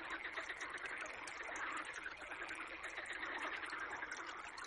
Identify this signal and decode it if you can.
Emergency Alert System (SAME) tones, heard as harsh buzzing bursts. It is heard from 0.5 to 4.2 s.